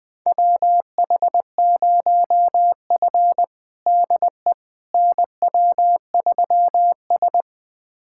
Morse code, heard as a keyed tone, about 20 wpm, 700 Hz. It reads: WH0F DE NW3S
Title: USA Callsign DE Callsign - 4 Characters - 20wpm